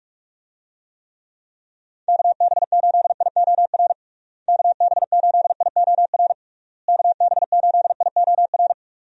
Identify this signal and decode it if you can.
Morse code, heard as a keyed tone, about 46 wpm, 700 Hz. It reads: KB8IOR KB8IOR KB8IOR
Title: USA Call Signs - 6 Characters - Repeated 3x - 45wpm